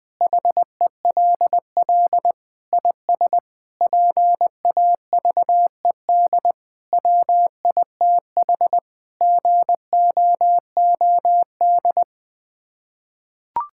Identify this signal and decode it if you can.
Morse code, heard as a keyed tone, about 20 wpm, 700 Hz. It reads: HELL IS PAVED WITH GOOD
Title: Sets of 8 Words - 20wpm